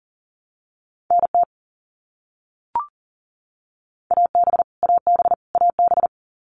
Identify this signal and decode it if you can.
Morse code, heard as a keyed tone, about 40 wpm, 700 Hz. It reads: NT AB AB AB
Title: Canadian Provinces and Territories - Repeated 3x - 40wpm